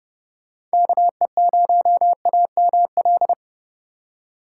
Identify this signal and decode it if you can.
Morse code, heard as a keyed tone, about 33 wpm, 700 Hz. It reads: KE0AML